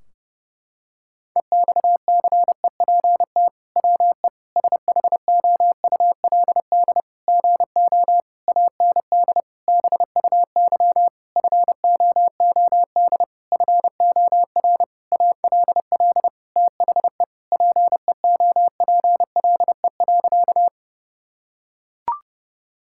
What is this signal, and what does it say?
Morse code, heard as a keyed tone, about 32 wpm, 700 Hz. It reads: EXCEPT WE SHOULD GO AND BUY FOOD FOR ALL THE PEOPLE.
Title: Sentences from Top 1600 Words 30wpm